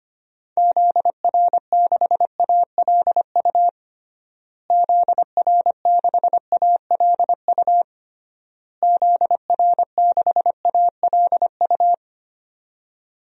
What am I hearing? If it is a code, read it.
Morse code, heard as a keyed tone, about 25 wpm, 700 Hz. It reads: ZR6ALU ZR6ALU ZR6ALU
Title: DXCC-100 Call Signs - 6 Characters - Repeated 3x - 25wpm